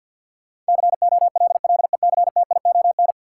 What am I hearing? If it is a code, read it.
Morse code, heard as a keyed tone, about 53 wpm, 700 Hz. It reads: COLLECTION
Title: Sentences from Top 1200 Words - New Word Drill Alt - 50wpm